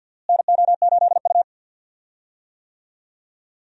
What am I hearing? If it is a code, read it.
Morse code, heard as a keyed tone, about 52 wpm, 700 Hz. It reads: NO8U
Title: USA Call Signs - 4 Characters - Repeated 3x - 50wpm